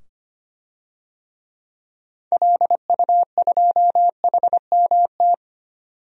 Morse code, heard as a keyed tone, about 25 wpm, 700 Hz. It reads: LU2HMT
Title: DXCC-100 Call Signs - 6 Characters - 25wpm